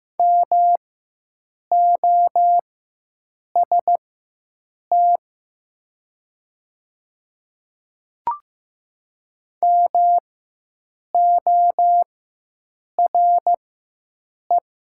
Morse code, heard as a keyed tone, about 15 wpm, spaced as for 7 wpm, 700 Hz. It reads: MOST MORE
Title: 4x Character Spacing - Top 300 Words - 15wpm